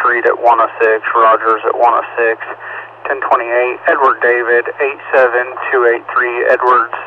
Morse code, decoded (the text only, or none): none